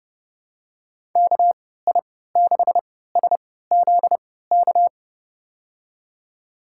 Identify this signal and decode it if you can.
Morse code, heard as a keyed tone, about 30 wpm, spaced as for 17 wpm, 700 Hz. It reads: KI6SZK
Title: USA Call Signs - 6 Characters - 3x Character Spacing - 30wpm